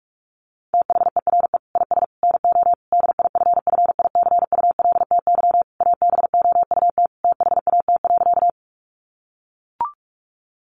Morse code, heard as a keyed tone, about 44 wpm, 700 Hz. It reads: THERE IS NO DIFFICULTY ABOUT THAT.
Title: Sentences from Top 1500 Words 45wpm